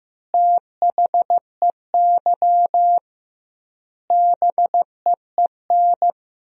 Morse code, heard as a keyed tone, about 15 wpm, 700 Hz. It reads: THEY BEEN
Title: 2x Word-Spacing - Sentences from Top 100 Words - 15wpm